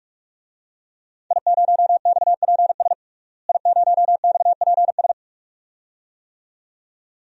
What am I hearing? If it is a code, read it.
Morse code, heard as a keyed tone, about 44 wpm, 700 Hz. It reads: I0XPS I0XPS
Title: DXCC-100 Call Signs - 5 Characters - Repeated 2x - 45wpm